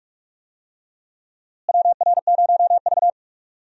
Morse code, heard as a keyed tone, about 46 wpm, 700 Hz. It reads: WR0V
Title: USA Call Signs - 4 Characters - Repeated 2x - 45wpm